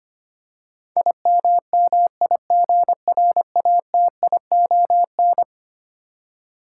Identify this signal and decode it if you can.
Morse code, heard as a keyed tone, about 25 wpm, 700 Hz. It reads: IMMIGRATION
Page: morse-code-ninja-practice-25wpm